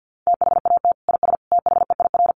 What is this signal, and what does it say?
Morse code, heard as a keyed tone, about 50 wpm, 700 Hz. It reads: THAT IS THEIR
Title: Sentences from Top 1300 Words 50wpm